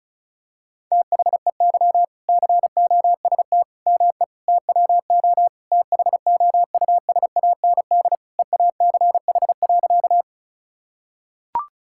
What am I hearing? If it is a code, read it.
Morse code, heard as a keyed tone, about 36 wpm, 700 Hz. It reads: THEY COST ME TWO THOUSAND EACH.
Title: Sentences from Top 1000 Words - Review Part 5 - 35wpm